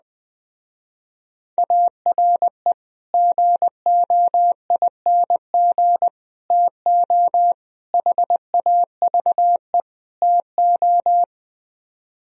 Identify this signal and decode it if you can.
Morse code, heard as a keyed tone, about 20 wpm, 700 Hz. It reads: ARE GOING TO HAVE TO